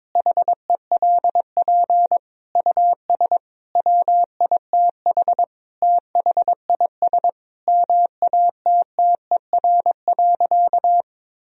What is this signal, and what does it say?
Morse code, heard as a keyed tone, about 22 wpm, 700 Hz. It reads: HELP US WITH THIS MATTER.
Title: Sentences from Top 1000 Words - Review Part 2 - 22wpm